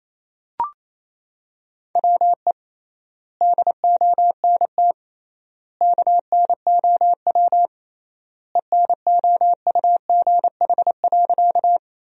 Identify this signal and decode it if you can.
Morse code, heard as a keyed tone, about 28 wpm, 700 Hz. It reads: WE DONT KNOW ENOUGH.
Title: 3x Word-Spacing - Sentences from Top 400 Words - 28wpm